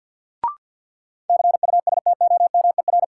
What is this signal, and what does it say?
Morse code, heard as a keyed tone, about 48 wpm, 700 Hz. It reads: CUSTOMER
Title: Sentences from Top 1100 Words - New Word Drill - 50wpm